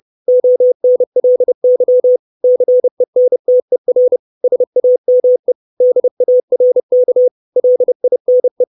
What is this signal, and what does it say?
Morse code, heard as a keyed tone, about 30 wpm, 500 Hz. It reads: ONLY CENTER SAME DARK LINE
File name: 632 Words 30WPM